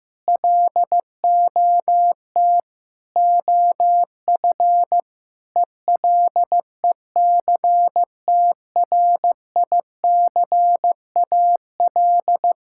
Morse code, heard as a keyed tone, about 16 wpm, 700 Hz. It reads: LOT OF ELECTRICAL
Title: Sentences from Top 1000 Words - Review Part 6 - 15wpm